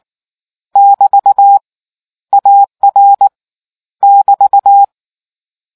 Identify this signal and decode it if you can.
Morse code, heard as a keyed tone, about 19 wpm, 800 Hz. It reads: <BT> AR <BT>